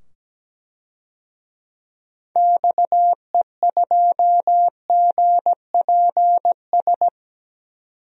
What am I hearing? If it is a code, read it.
Morse code, heard as a keyed tone, about 17 wpm, 700 Hz. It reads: XE2GPS
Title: DXCC-100 Call Signs - 6 Characters - 17wpm